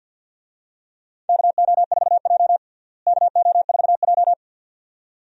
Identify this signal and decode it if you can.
Morse code, heard as a keyed tone, about 49 wpm, 700 Hz. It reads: KO4J KO4J
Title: USA Call Signs - 4 Characters - Repeated 2x - 50wpm